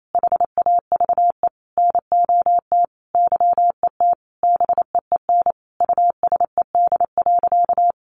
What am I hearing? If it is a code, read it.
Morse code, heard as a keyed tone, about 28 wpm, 700 Hz. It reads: HAVE NOT YET BEEN USED.
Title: Sentences from Top 700 Words 28wpm